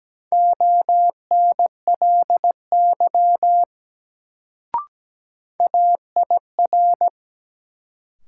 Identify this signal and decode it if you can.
Morse code, heard as a keyed tone, about 17 wpm, 700 Hz. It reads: ONLY AIR
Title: Sentences from Top 200 Words - New Word Drill - 17wpm